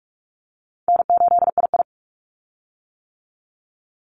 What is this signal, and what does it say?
Morse code, heard as a keyed tone, about 46 wpm, 700 Hz. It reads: N8II
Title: Easy Call Signs - 45wpm